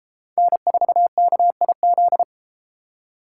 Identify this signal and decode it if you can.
Morse code, heard as a keyed tone, about 30 wpm, 700 Hz. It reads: N4KIZ